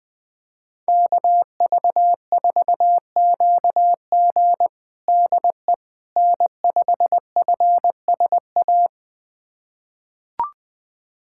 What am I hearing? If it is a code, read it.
Morse code, heard as a keyed tone, about 21 wpm, 700 Hz. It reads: KV4QG DE N5FSA
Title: USA Callsign DE Callsign - 5 Characters - 20wpm